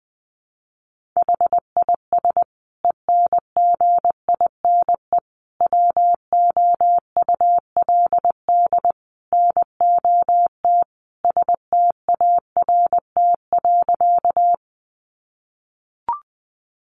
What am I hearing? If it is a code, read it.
Morse code, heard as a keyed tone, about 20 wpm, 700 Hz. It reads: HIS ENGINE WOULD NOT START.
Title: Sentences from Top 1500 Words 20wpm